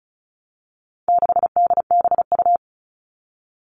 Morse code, heard as a keyed tone, about 34 wpm, 700 Hz. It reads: 6DBU
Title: All Cut Numbers - 4-Digit 35wpm